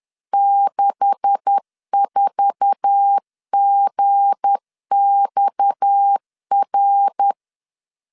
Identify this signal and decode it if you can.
Morse code, heard as a keyed tone, about 10 wpm, 800 Hz. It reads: 64GXR
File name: CW / DET / 1 9 1500C S A 10wpm 4511